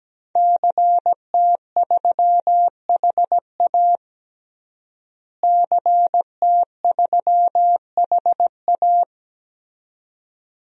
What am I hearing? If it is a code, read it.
Morse code, heard as a keyed tone, about 17 wpm, 700 Hz. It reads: CT3HA CT3HA
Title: DXCC-100 Call Signs - 5 Characters - Repeated 2x - 17wpm